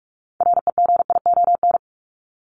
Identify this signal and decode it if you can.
Morse code, heard as a keyed tone, about 44 wpm, 700 Hz. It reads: REGION
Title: Top 1251-1500 Words 45wpm